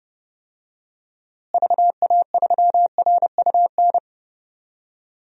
Morse code, heard as a keyed tone, about 30 wpm, 700 Hz. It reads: VA3RUN
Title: DXCC-100 Call Signs - 6 Characters - Repeated 3x - 30wpm